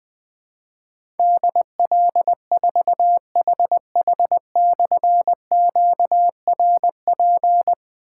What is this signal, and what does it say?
Morse code, heard as a keyed tone, about 20 wpm, 700 Hz. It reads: DL4HH/QRP
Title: Difficult Call Signs 20wpm